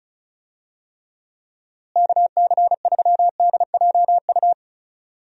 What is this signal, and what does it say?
Morse code, heard as a keyed tone, about 36 wpm, 700 Hz. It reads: KC3DJU
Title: USA Call Signs - 6 Characters - Repeated 3x - 35wpm